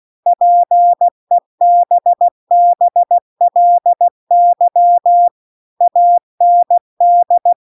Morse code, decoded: PEBBLY AND

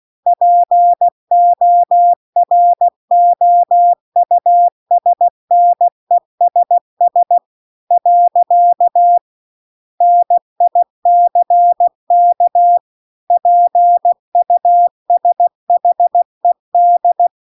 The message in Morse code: POROUSNESS . NICK PUSHED

